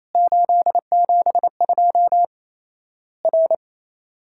Morse code, heard as a keyed tone, about 27 wpm, 700 Hz. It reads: 872 R
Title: Worked All Europe DX Contest - QTC Exchange - 28wpm